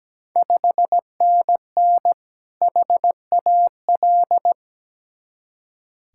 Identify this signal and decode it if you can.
Morse code, heard as a keyed tone, about 17 wpm, 690 Hz. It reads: 5NN HAL
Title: Ontario QSO Party Exchange 17wpm